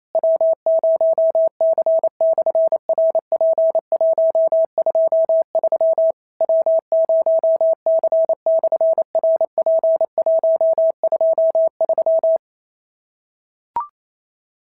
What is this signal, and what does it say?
Morse code, heard as a keyed tone, about 28 wpm, 650 Hz. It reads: W0C/RP123 W0C/RP123
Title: SOTA QSO Practice 28wpm